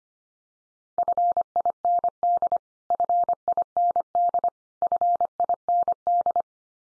Morse code, heard as a keyed tone, about 25 wpm, 700 Hz. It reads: FIND FIND FIND
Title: Top 100 Words 25wpm x3